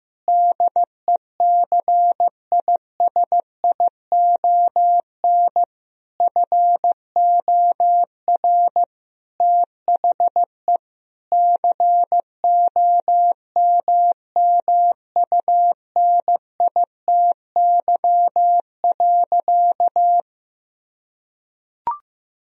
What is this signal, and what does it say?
Morse code, heard as a keyed tone, about 15 wpm, 700 Hz. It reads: DECISION FOR THE COMMUNITY.
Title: Sentences from Top 1000 Words - Review Part 4 - 15wpm